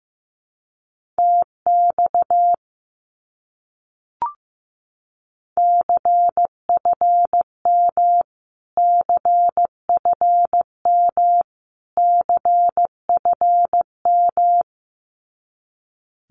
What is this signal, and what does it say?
Morse code, heard as a keyed tone, about 15 wpm, 700 Hz. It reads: TX CFM CFM CFM
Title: Abbreviations Repeated 3x - 15wpm